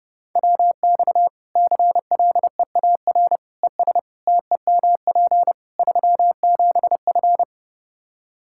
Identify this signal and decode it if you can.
Morse code, heard as a keyed tone, about 32 wpm, 700 Hz. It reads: WX CLEAR ES TEMP 37F
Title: K1USN SST QSO Practice 30wpm